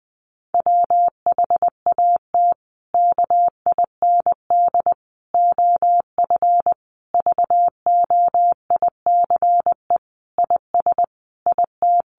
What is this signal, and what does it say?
Morse code, heard as a keyed tone, about 20 wpm, 700 Hz. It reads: WHAT KIND OF VOICE IS IT
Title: Sentences from Top 500 Words 20wpm